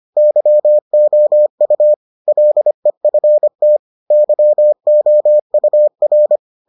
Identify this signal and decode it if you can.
Morse code, heard as a keyed tone, about 25 wpm, 600 Hz. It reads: YOU LEFT YOUR